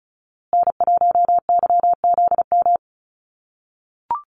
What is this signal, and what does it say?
Morse code, heard as a keyed tone, about 34 wpm, 700 Hz. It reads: N1YZM
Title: USA Call Signs Repeated 3x 35wpm